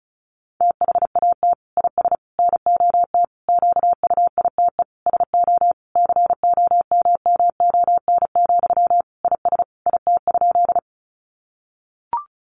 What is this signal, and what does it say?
Morse code, heard as a keyed tone, about 34 wpm, 700 Hz. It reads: THAT IS NOT QUITE SO COMMON, IS IT?
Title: Sentences from Top 700 Words 35wpm